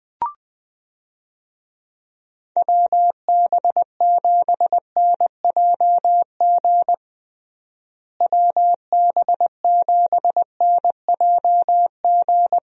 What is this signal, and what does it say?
Morse code, heard as a keyed tone, about 20 wpm, 700 Hz. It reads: WB7NJG WB7NJG
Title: USA Call Signs - 6 Characters - Repeated 3x - 20wpm